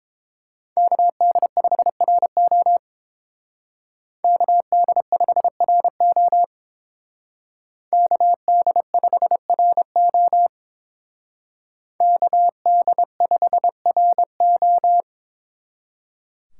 Morse code, heard as a keyed tone, about 27 wpm, 700 Hz. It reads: KD5RO KD5RO KD5RO KD5RO
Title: Speed Racing - USA 2x2 Call Signs 22wpm